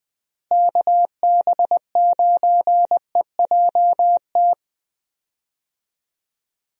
Morse code, heard as a keyed tone, about 20 wpm, 700 Hz. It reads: KB9EJT
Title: USA Call Signs - 6 Characters - Repeated 3x - 20wpm